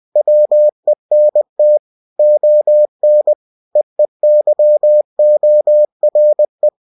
Morse code, heard as a keyed tone, about 20 wpm, 600 Hz. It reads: WENT ON EEYORE
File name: cw-books / winnie-the-pooh / winnie-20wpm-page-0045